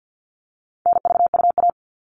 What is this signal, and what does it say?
Morse code, heard as a keyed tone, about 50 wpm, 700 Hz. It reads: NVUA
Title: All Cut Numbers - 4-Digit 50wpm